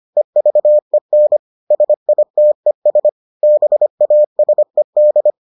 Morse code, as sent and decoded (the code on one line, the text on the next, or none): . ...- . -. / ... .. - . ... / -... .- ... -..
EVEN SITES BASED